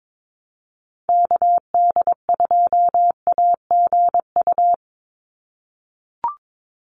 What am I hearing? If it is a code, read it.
Morse code, heard as a keyed tone, about 22 wpm, 700 Hz. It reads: KD2AGU
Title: USA Call Signs 22wpm